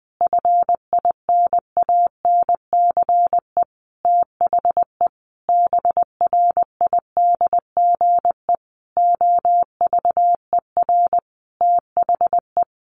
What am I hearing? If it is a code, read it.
Morse code, heard as a keyed tone, about 20 wpm, 700 Hz. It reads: FINANCE THE BRIDGE OVER THE
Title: Sentences from Top 1500 Words 20wpm